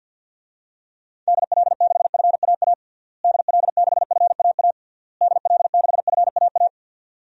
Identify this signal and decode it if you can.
Morse code, heard as a keyed tone, about 49 wpm, 700 Hz. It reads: DL6FAA DL6FAA DL6FAA
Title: DXCC-100 Call Signs - 6 Characters - Repeated 3x - 50wpm